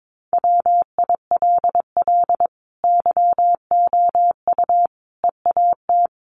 Morse code, heard as a keyed tone, about 22 wpm, 700 Hz. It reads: WILL YOU EAT
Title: Sentences from Top 1300 Words 22wpm